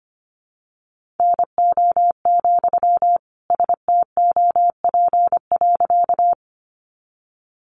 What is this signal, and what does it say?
Morse code, heard as a keyed tone, about 25 wpm, 700 Hz. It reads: NO, STOP.
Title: Sentences from Top 300 Words 25wpm